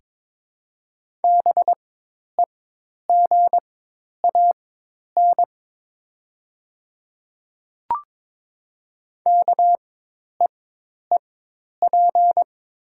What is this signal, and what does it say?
Morse code, heard as a keyed tone, about 22 wpm, spaced as for 10 wpm, 700 Hz. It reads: BEGAN KEEP
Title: 4x Character Spacing - Top 300 Words - 22wpm